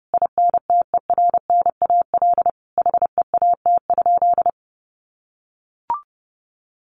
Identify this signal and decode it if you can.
Morse code, heard as a keyed tone, about 30 wpm, 700 Hz. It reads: INTERNAL HEAT?
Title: Sentences from Top 1400 Words 30wpm